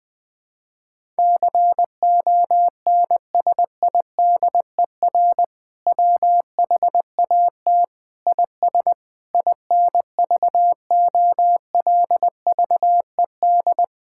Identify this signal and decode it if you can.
Morse code, heard as a keyed tone, about 20 wpm, 700 Hz. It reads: CONSIDER WHAT IS INVOLVED
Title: Sentences from Top 1000 Words - Review Part 5 - 20wpm